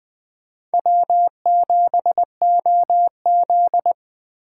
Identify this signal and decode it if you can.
Morse code, heard as a keyed tone, about 20 wpm, 700 Hz. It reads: W7OZ
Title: USA Call Signs - 4 Characters - 3x Character Spacing - 20wpm